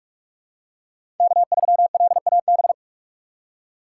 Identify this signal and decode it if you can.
Morse code, heard as a keyed tone, about 44 wpm, 700 Hz. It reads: K3LAB
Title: USA Call Signs - 6x Character Spacing - 45wpm